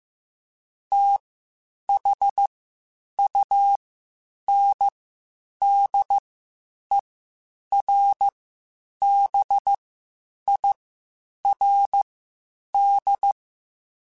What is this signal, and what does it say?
Morse code, heard as a keyed tone, about 15 wpm, spaced as for 8 wpm, 800 Hz. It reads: THUNDERBIRD
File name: QSOs / QSOs-mp3s / WD0BGP 1@10